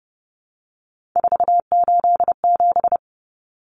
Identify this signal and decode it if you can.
Morse code, heard as a keyed tone, about 30 wpm, 700 Hz. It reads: 487
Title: Common Cut Numbers - 3-Digit 30wpm